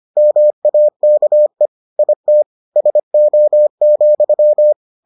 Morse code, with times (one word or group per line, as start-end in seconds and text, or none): 0.2-1.7 MAKE
2.0-2.4 IT
2.8-4.7 SO,